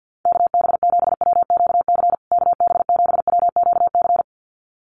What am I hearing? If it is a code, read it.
Morse code, heard as a keyed tone, about 50 wpm, 700 Hz. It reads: KB7WQC KB7WQC
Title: USA Call Signs Repeated 2x 50wpm